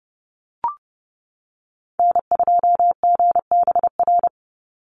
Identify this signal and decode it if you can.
Morse code, heard as a keyed tone, about 30 wpm, 700 Hz. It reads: N2GBR
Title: NA SOTA Activator Callsigns 30wpm